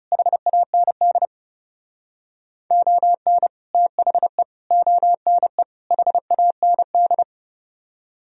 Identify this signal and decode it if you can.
Morse code, heard as a keyed tone, about 36 wpm, 700 Hz. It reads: HAND ON THE ONE HAND
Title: Speed-Racing - Sets of 4 Words 30wpm